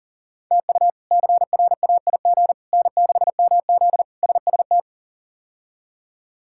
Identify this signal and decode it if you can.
Morse code, heard as a keyed tone, about 40 wpm, 700 Hz. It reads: TU CRAIG N6MZ SST
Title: K1USN SST QSO Practice - 40wpm